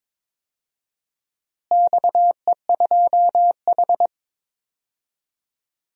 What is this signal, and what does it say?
Morse code, heard as a keyed tone, about 22 wpm, 700 Hz. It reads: XE2H
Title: DXCC-100 Call Signs - 4 Characters - Repeated 2x - 22wpm